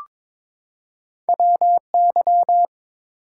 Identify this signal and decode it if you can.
Morse code, heard as a keyed tone, about 22 wpm, 700 Hz. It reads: WY
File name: ARRL Sections v2 - 22wpm